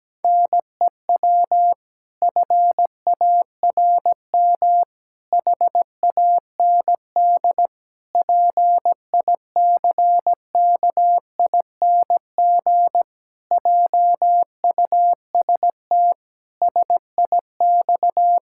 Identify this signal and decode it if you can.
Morse code, heard as a keyed tone, about 17 wpm, 700 Hz. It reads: NEW FARM HAND PICKING JUST SIX
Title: Pangram Challenge v2 - 17wpm